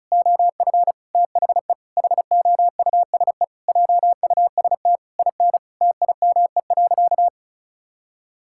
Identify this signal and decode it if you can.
Morse code, heard as a keyed tone, about 36 wpm, 700 Hz. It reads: OF THE HOUSE JUST IN TIME.
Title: Sentences from Top 1600 Words 35wpm